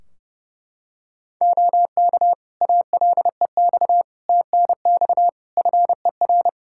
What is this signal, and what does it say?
Morse code, heard as a keyed tone, about 30 wpm, 700 Hz. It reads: OK ALEX TNX FER
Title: Practice QSO - Exchange 4 - Ending - 30wpm